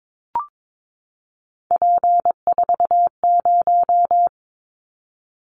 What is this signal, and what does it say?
Morse code, heard as a keyed tone, about 22 wpm, 700 Hz. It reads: P40
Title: Callsign Prefixes for Top 100 DXCC - 22wpm